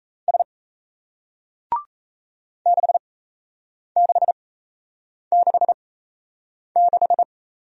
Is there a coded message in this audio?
Yes: Morse code from 0.3 to 7.2 s, heard as a keyed tone, about 37 wpm, spaced as for 8 wpm, 700 Hz. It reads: S 6666